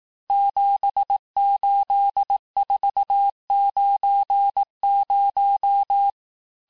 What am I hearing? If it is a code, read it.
Morse code, heard as a keyed tone, about 18 wpm, 800 Hz. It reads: 78490
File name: cours-du-lundi files / cours-1852-texte-chiffre-vitesse-17MPM